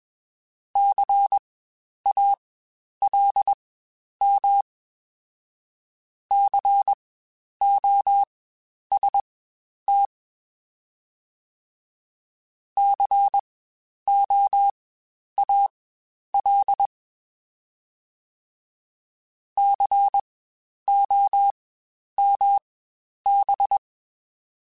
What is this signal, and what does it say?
Morse code, heard as a keyed tone, about 21 wpm, spaced as for 10 wpm, 800 Hz. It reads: CALM COST COAL COMB
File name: Spaced Alphas / Spaced Alphas mp3s / Cs sp @20